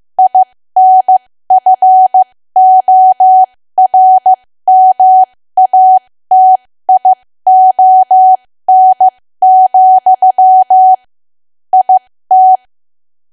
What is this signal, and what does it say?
Morse code, heard as a keyed tone, about 15 wpm, spaced as for 13 wpm, 750 Hz. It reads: INFORMATION, IT
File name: cwa / W1AW-Level2 / 13WPM